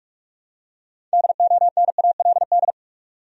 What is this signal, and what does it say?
Morse code, heard as a keyed tone, about 46 wpm, 700 Hz. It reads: DONALD
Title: USA Common Names - 45wpm